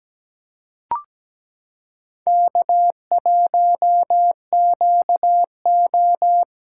Morse code, heard as a keyed tone, about 17 wpm, 700 Hz. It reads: K1QO
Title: K1USN SST QSO Practice - 17wpm